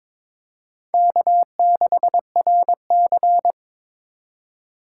Morse code, heard as a keyed tone, about 21 wpm, 700 Hz. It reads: K6RC
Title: K1USN SST QSO Practice - 22wpm